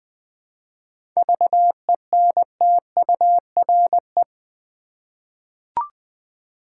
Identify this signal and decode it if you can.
Morse code, heard as a keyed tone, about 20 wpm, 700 Hz. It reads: VENTURE
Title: 7 Letter Words - First 3 Letters Given - 20wpm